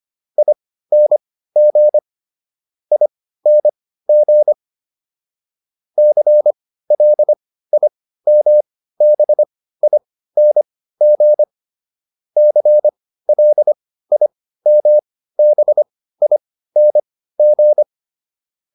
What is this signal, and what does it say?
Morse code, heard as a keyed tone, about 25 wpm, spaced as for 15 wpm, 600 Hz. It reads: ING ING CLIMBING CLIMBING